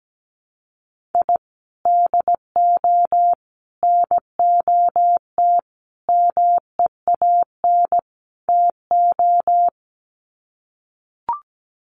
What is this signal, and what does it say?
Morse code, heard as a keyed tone, about 17 wpm, 700 Hz. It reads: I DO NOT MEAN TO
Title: Sets of 5 Words 17wpm